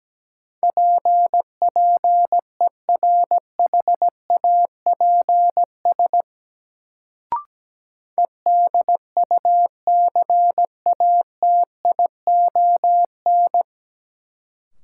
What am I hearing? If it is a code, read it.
Morse code, heard as a keyed tone, about 17 wpm, 700 Hz. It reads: PPERHAPS EDUCATION
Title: Sentences from Top 1100 Words - New Word Drill Alt - 17wpm